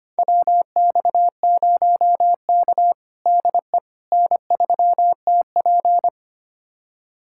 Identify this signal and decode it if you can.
Morse code, heard as a keyed tone, about 25 wpm, 700 Hz. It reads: WX0K DE N3TP
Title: USA Callsign DE Callsign - 4 Characters - 25wpm